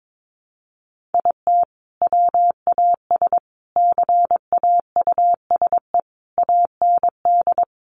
Morse code, heard as a keyed tone, about 22 wpm, 700 Hz. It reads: IT WAS CAUSE AND
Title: Sentences from Top 1000 Words 22wpm